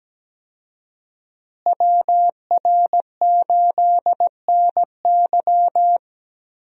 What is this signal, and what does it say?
Morse code, heard as a keyed tone, about 17 wpm, 700 Hz. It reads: WR8NY
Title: USA Call Signs - 5 Characters - Repeated 2x - 17wpm